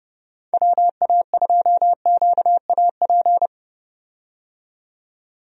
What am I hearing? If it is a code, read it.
Morse code, heard as a keyed tone, about 30 wpm, 700 Hz. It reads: WA2QAP
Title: USA Call Signs - 6 Characters - 30wpm